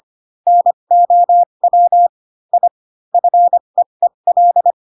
Morse code, heard as a keyed tone, about 25 wpm, spaced as for 22 wpm, 700 Hz. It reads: NOW I FEEL
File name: Pinocchio / Pinocchio ch7 18 25 ik1hge